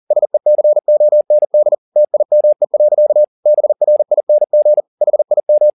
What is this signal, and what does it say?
Morse code, heard as a keyed tone, about 40 wpm, 600 Hz. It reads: SECOND TIME. BRING HIM
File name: the old man and the sea 2 40wpm